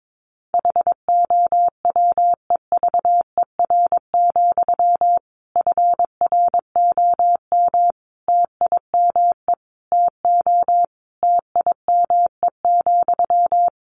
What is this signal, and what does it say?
Morse code, heard as a keyed tone, about 22 wpm, 700 Hz. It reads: HOWEVER, FROM TIME TO TIME,
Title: Sentences from Top 1300 Words 22wpm